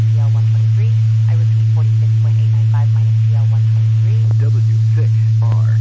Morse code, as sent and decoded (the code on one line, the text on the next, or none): none